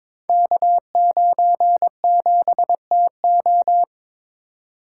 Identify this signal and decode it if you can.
Morse code, heard as a keyed tone, about 22 wpm, 700 Hz. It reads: K97TO
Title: Code Groups - 5 Letter:Numbers - 22wpm